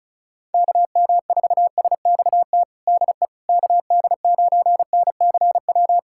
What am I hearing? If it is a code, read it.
Morse code, heard as a keyed tone, about 36 wpm, 700 Hz. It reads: KM4SXT DE KD9NCW
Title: USA Callsign DE Callsign - 6 Characters - 35wpm